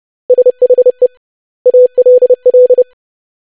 Morse code, heard as a keyed tone, about 30 wpm, 500 Hz. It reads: SHE ALL
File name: lesson / web / basic100-25wpm